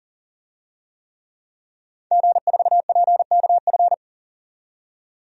Morse code, heard as a keyed tone, about 40 wpm, 700 Hz. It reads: G4PKF
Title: DXCC-100 Call Signs - 5 Characters - 40wpm